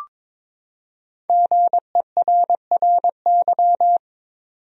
Morse code, heard as a keyed tone, about 22 wpm, 700 Hz. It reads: GERRY
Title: USA Common Names - 22wpm